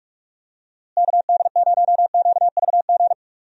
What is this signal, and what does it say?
Morse code, heard as a keyed tone, about 41 wpm, 700 Hz. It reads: KD0QVG
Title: USA Call Signs - 6 Characters - Repeated 3x - 45wpm